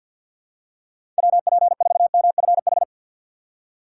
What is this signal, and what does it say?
Morse code, heard as a keyed tone, about 47 wpm, 700 Hz. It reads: W P 4 M F H
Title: USA Call Signs - 6 Characters - 6x Character Spacing - 50wpm